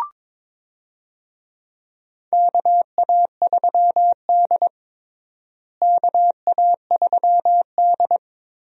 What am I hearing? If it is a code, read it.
Morse code, heard as a keyed tone, about 21 wpm, 700 Hz. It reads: KA3D KA3D
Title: USA Call Signs - 4 Characters - Repeated 2x - 22wpm